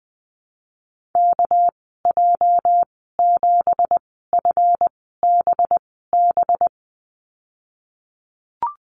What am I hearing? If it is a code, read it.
Morse code, heard as a keyed tone, about 19 wpm, spaced as for 14 wpm, 700 Hz. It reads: KJ7FBB
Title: 231 Lesson 37g - Letter x - Call Signs 20wpm